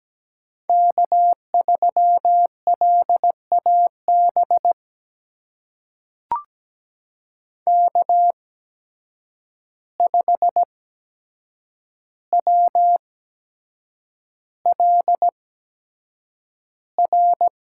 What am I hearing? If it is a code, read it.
Morse code, heard as a keyed tone, about 17 wpm, spaced as for 7 wpm, 700 Hz. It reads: K3LAB K5WLR
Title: USA Call Signs - 5 Characters - 8x Character Spacing - 17wpm